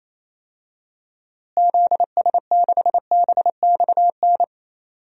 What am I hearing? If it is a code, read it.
Morse code, heard as a keyed tone, about 28 wpm, 700 Hz. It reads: ZS6BXN
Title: DXCC-100 Call Signs - 6 Characters - Repeated 3x - 28wpm